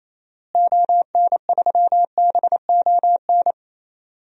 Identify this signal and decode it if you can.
Morse code, heard as a keyed tone, about 28 wpm, 700 Hz. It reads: ON3BON
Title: DXCC-100 Call Signs - 6 Characters - Repeated 2x - 28wpm